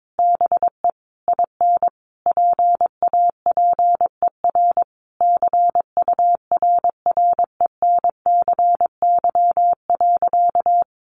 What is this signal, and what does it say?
Morse code, heard as a keyed tone, about 22 wpm, 700 Hz. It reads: BE IN PAPER CURRENCY.
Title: Sentences from Top 1500 Words 22wpm